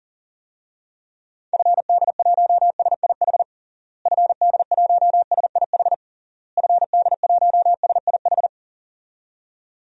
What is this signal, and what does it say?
Morse code, heard as a keyed tone, about 40 wpm, 700 Hz. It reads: FD1SIH FD1SIH FD1SIH
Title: DXCC-100 Call Signs - 6 Characters - Repeated 3x - 40wpm